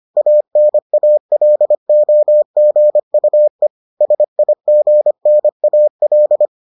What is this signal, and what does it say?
Morse code, heard as a keyed tone, about 25 wpm, 600 Hz. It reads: ANALOGUE SIGNAL